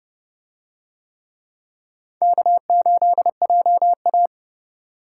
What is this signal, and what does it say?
Morse code, heard as a keyed tone, about 27 wpm, 700 Hz. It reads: K8JA